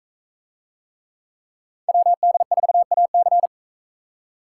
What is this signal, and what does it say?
Morse code, heard as a keyed tone, about 43 wpm, 700 Hz. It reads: WD4AC